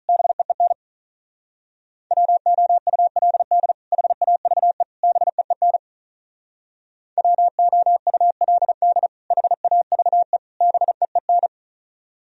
Morse code, heard as a keyed tone, about 44 wpm, 700 Hz. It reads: BEEN WOULD HAVE BEEN WOULD HAVE BEEN
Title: Speed-Racing - Sets of 3 Words 35wpm